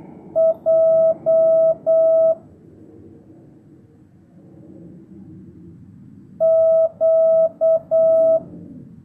Morse code, heard as a keyed tone, about 8 wpm, 650 Hz. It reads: J Q
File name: jh8cbh / 2025cw